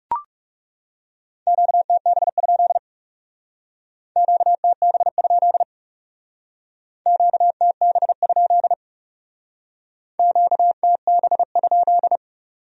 Morse code, heard as a keyed tone, about 36 wpm, 700 Hz. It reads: QTB? QTB? QTB? QTB?
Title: Speed-Racing Q-Codes - 30wpm